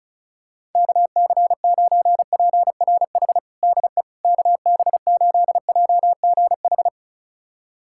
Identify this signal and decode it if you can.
Morse code, heard as a keyed tone, about 36 wpm, 700 Hz. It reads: KC9PRH DE KB8JGH
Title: USA Callsign DE Callsign - 6 Characters - 35wpm